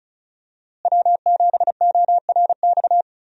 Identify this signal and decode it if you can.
Morse code, heard as a keyed tone, about 36 wpm, 700 Hz. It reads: W7ORX
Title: USA Call Signs - 5 Characters - Repeated 2x - 35wpm